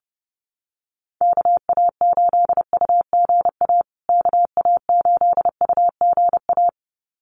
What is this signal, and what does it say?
Morse code, heard as a keyed tone, about 30 wpm, 700 Hz. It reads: KA8UGA KA8UGA
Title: USA Call Signs Repeated 2x 30wpm